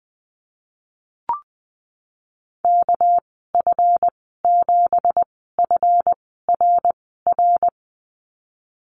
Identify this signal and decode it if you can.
Morse code, heard as a keyed tone, about 19 wpm, spaced as for 14 wpm, 700 Hz. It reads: KF7FRR
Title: 238 Lesson 38g - Letter q - Call Signs 20wpm